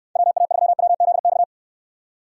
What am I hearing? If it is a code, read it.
Morse code, heard as a keyed tone, about 65 wpm, 700 Hz. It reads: WA3WZX